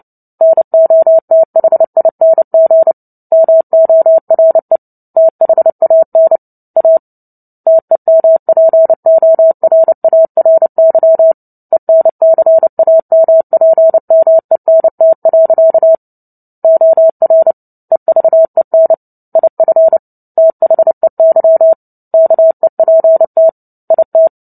Morse code, heard as a keyed tone, about 29 wpm, 650 Hz. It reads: NOTHING MORE THAN A TEMPORARY ENCAMPMENT. OR EVEN IF THEY KEPT IT
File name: Book Two / 03-Chapter 3 - 29 WPM